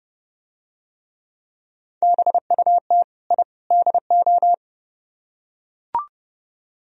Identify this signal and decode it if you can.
Morse code, heard as a keyed tone, about 29 wpm, 700 Hz. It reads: BUT I DO